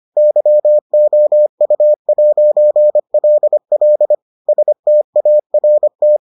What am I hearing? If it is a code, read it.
Morse code, heard as a keyed tone, about 25 wpm, 600 Hz. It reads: YOU'LL START